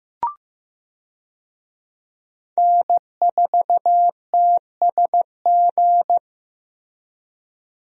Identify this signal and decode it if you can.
Morse code, heard as a keyed tone, about 15 wpm, 700 Hz. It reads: N4TSG
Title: USA Call Signs - 5 Characters - Repeated 3x - 15wpm